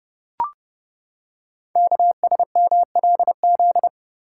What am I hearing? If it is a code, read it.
Morse code, heard as a keyed tone, about 30 wpm, 700 Hz. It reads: KSMLZ
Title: Code Groups - 5 Letters - 30wpm